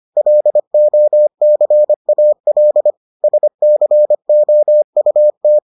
Morse code, heard as a keyed tone, about 25 wpm, 600 Hz. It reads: LOCAL SCOUT